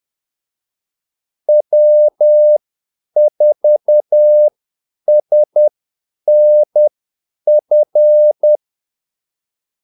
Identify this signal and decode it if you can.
Morse code, heard as a keyed tone, about 10 wpm, spaced as for 8 wpm, 600 Hz. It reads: W4SNF
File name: Callsigns-10-8wpm